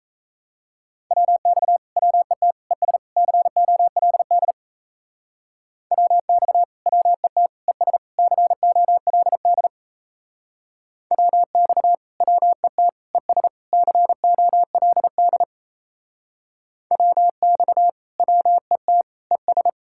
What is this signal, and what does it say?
Morse code, heard as a keyed tone, about 34 wpm, 700 Hz. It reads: WX WET ES COLD WX WET ES COLD WX WET ES COLD WX WET ES
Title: Speed-Racing Common QSO Phrases Encore - 28wpm